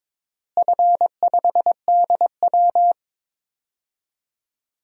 Morse code, heard as a keyed tone, about 22 wpm, 700 Hz. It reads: F5DW
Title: DXCC-100 Call Signs - 4 Characters - Repeated 3x - 22wpm